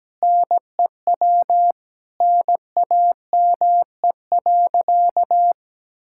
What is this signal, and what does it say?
Morse code, heard as a keyed tone, about 17 wpm, 700 Hz. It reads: NEW NAME.
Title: Sentences from Top 1000 Words - Review Part 5 - 17wpm